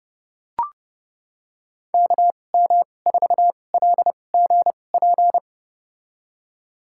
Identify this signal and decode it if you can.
Morse code, heard as a keyed tone, about 32 wpm, spaced as for 22 wpm, 700 Hz. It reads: KM4LGP
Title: USA Call Signs - 2x Character Spacing - 30wpm